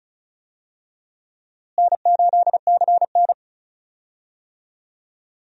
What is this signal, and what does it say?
Morse code, heard as a keyed tone, about 36 wpm, 700 Hz. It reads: N8CN
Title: USA Call Signs - 4 Characters - Repeated 2x - 35wpm